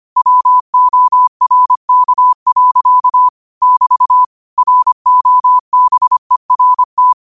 Morse code, heard as a keyed tone, about 25 wpm, 1000 Hz. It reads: WORK. <BT> ROBERT